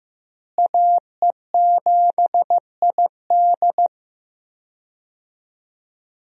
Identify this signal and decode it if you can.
Morse code, heard as a keyed tone, about 15 wpm, 700 Hz. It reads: AE7ID
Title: USA Call Signs - 5 Characters - Repeated 3x - 15wpm